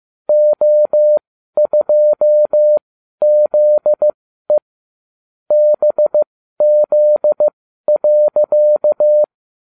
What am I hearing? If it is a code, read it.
Morse code, heard as a keyed tone, about 15 wpm, spaced as for 12 wpm, 600 Hz. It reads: O2ZE BZ.